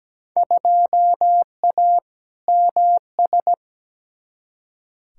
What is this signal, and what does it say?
Morse code, heard as a keyed tone, about 17 wpm, 700 Hz. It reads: 2A MS
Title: Field Day Exchanges - 17wpm